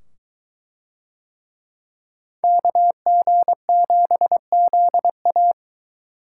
Morse code, heard as a keyed tone, about 20 wpm, 700 Hz. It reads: KG7ZA